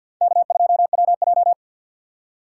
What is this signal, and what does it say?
Morse code, heard as a keyed tone, about 50 wpm, 700 Hz. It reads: K2WJ
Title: USA Call Signs - 4 Characters - 6x Character Spacing - 50wpm